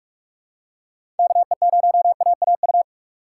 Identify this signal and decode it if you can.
Morse code, heard as a keyed tone, about 44 wpm, 700 Hz. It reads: KE0AAU
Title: USA Call Signs - 4x Character Spacing - 45wpm